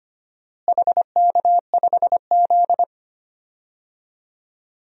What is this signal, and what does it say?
Morse code, heard as a keyed tone, about 25 wpm, 700 Hz. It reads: HK5Z
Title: DXCC-100 Call Signs - 4 Characters - Repeated 2x - 25wpm